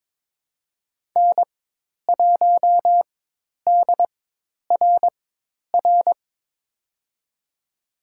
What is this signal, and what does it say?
Morse code, heard as a keyed tone, about 22 wpm, spaced as for 10 wpm, 700 Hz. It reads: N1DRR
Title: USA Call Signs - 5 Characters - 4x Character Spacing - 22wpm